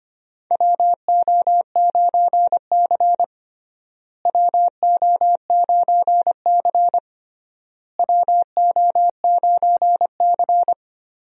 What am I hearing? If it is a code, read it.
Morse code, heard as a keyed tone, about 25 wpm, 700 Hz. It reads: WO9C WO9C WO9C
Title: USA Call Signs - 4 Characters - Repeated 3x - 25wpm